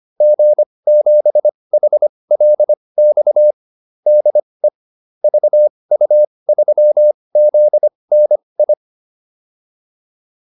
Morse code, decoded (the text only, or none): G7HLX DE VU3ZNI